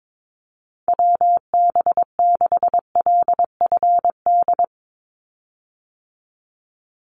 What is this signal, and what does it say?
Morse code, heard as a keyed tone, about 22 wpm, 700 Hz. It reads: WB6LFD
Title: USA Call Signs 22wpm